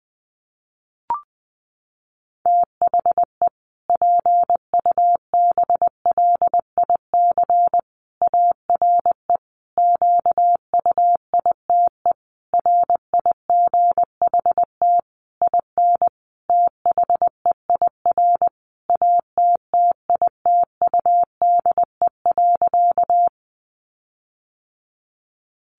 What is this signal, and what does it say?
Morse code, heard as a keyed tone, about 20 wpm, 700 Hz. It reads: THE PUBLIC ARE QUITE RIGHT IN THEIR ATTITUDE.
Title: Sentences from Top 1300 Words 20wpm